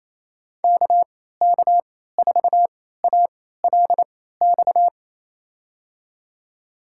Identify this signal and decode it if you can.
Morse code, heard as a keyed tone, about 28 wpm, spaced as for 16 wpm, 700 Hz. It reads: KK4ALX
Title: USA Call Signs - 6 Characters - 3x Character Spacing - 28wpm